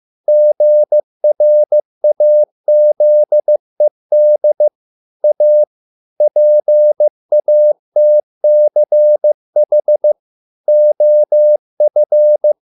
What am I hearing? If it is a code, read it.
Morse code, heard as a keyed tone, about 15 wpm, 600 Hz. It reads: GRAZED A PATCH OF